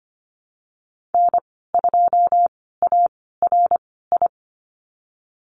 Morse code, heard as a keyed tone, about 25 wpm, spaced as for 16 wpm, 700 Hz. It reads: N2ARI
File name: Lesson 15g - Number 5 - Call Signs 25wpm